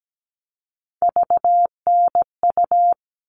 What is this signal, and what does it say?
Morse code, heard as a keyed tone, about 17 wpm, 700 Hz. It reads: VNU
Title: All Cut Numbers - 3-Digit 17wpm